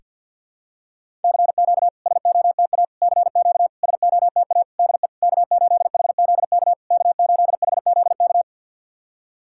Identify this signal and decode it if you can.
Morse code, heard as a keyed tone, about 50 wpm, 700 Hz. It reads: CQ SOTA CQ SOTA DE K8HZK K8HZK
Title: SOTA QSO Practice 50wpm